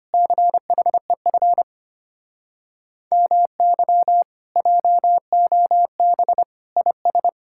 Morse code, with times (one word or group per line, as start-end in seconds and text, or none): 0.1-1.6 CHEF
3.1-4.2 MY
4.6-6.4 JOB
6.8-7.3 IS